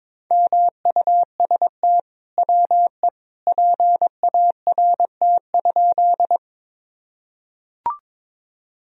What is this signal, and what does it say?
Morse code, heard as a keyed tone, about 23 wpm, 700 Hz. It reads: MUST WE PART?
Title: Sentences from Top 1000 Words - Review Part 1 - 22wpm